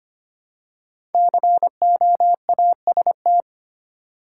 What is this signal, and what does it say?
Morse code, heard as a keyed tone, about 25 wpm, 700 Hz. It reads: COAST
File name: 5 Letter Words - First 3 Letters Given - 25wpm